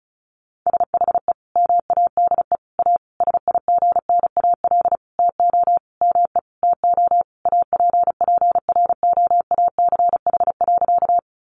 Morse code, heard as a keyed tone, about 34 wpm, 700 Hz. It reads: SHE MADE A SIGNAL TO ME TO APPROACH.
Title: Sentences from Top 1400 Words 35wpm